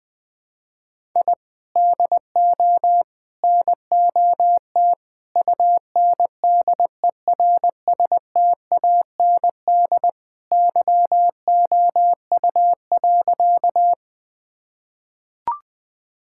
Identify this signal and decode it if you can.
Morse code, heard as a keyed tone, about 20 wpm, 700 Hz. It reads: I DO NOT UNDERSTAND YOU.
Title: Sentences with 4-Word Phrases - Spoken Beginning - 20wpm